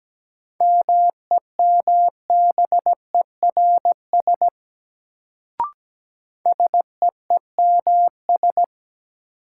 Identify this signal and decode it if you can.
Morse code, heard as a keyed tone, about 17 wpm, 700 Hz. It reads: MEMBERS SEEMS
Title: Sets of 3 Words - New Word Drill - 17wpm